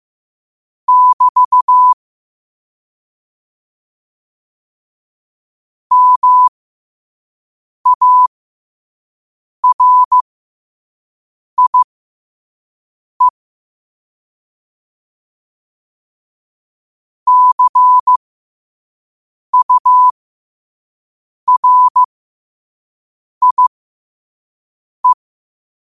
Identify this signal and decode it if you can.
Morse code, heard as a keyed tone, about 15 wpm, spaced as for 5 wpm, 1000 Hz. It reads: <BT> MARIE CURIE
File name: qotdfiles / 5wpm